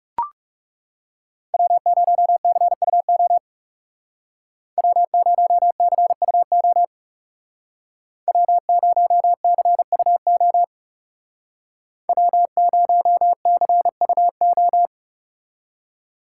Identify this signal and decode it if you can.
Morse code, heard as a keyed tone, about 38 wpm, 700 Hz. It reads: W0CUO W0CUO W0CUO W0CUO